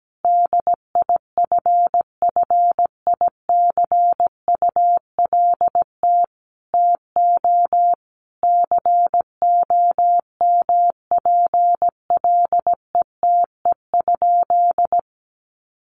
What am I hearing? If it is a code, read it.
Morse code, heard as a keyed tone, about 17 wpm, 700 Hz. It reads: DIFFICULT TO COMPLETE?
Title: Sentences from Top 1500 Words 17wpm